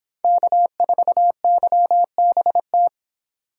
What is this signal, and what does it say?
Morse code, heard as a keyed tone, about 29 wpm, 700 Hz. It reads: K4YBT